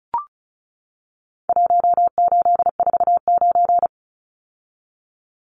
Morse code, heard as a keyed tone, about 34 wpm, 700 Hz. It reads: 1849
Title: Numbers - Contest Style - 4-Digit 35wpm